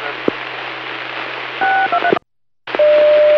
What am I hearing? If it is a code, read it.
DTMF, heard as telephone touch tones, tones of about 80 ms. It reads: 623